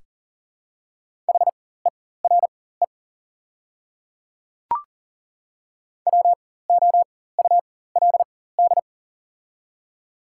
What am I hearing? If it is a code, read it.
Morse code, heard as a keyed tone, about 40 wpm, spaced as for 19 wpm, 700 Hz. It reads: HERE WOULD
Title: 4x Character Spacing - Top 200 Words - 40wpm